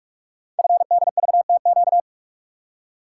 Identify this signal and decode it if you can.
Morse code, heard as a keyed tone, about 44 wpm, 700 Hz. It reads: FDVTQ
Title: Code Groups - 5 Letters - 45wpm